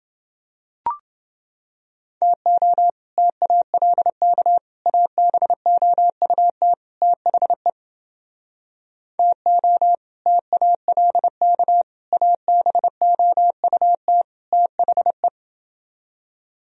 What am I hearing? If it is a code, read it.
Morse code, heard as a keyed tone, about 30 wpm, 700 Hz. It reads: TO TALK ABOUT THE TO TALK ABOUT THE